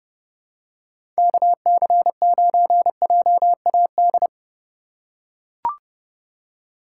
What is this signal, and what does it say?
Morse code, heard as a keyed tone, about 29 wpm, 700 Hz. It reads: KC9JAD
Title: USA Call Signs - 4x Character Spacing - 30wpm